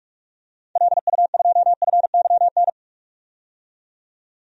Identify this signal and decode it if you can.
Morse code, heard as a keyed tone, about 46 wpm, 700 Hz. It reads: LU2FYN
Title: DXCC-100 Call Signs - 6 Characters - 45wpm